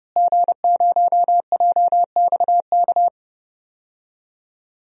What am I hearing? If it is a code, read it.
Morse code, heard as a keyed tone, about 32 wpm, 700 Hz. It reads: G0JXK
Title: DXCC-100 Call Signs - 5 Characters - Repeated 2x - 30wpm